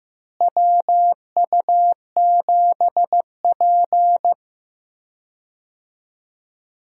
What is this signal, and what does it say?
Morse code, heard as a keyed tone, about 15 wpm, 700 Hz. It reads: WU7P
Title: USA Call Signs - 4 Characters - Repeated 2x - 15wpm